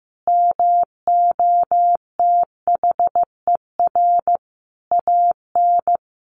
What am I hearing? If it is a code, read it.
Morse code, heard as a keyed tone, about 15 wpm, 700 Hz. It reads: MOTHER AN
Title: Sentences from Top 1500 Words 15wpm